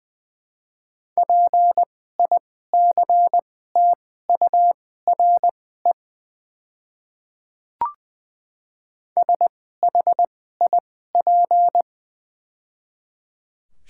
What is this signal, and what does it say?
Morse code, heard as a keyed tone, about 21 wpm, spaced as for 15 wpm, 690 Hz. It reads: PICTURE SHIP
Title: 2x Character Spacing - Top 500 Words - 20wpm